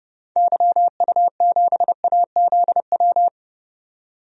Morse code, heard as a keyed tone, about 30 wpm, 700 Hz. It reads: YU7AZW
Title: DXCC-100 Call Signs - 6 Characters - 30wpm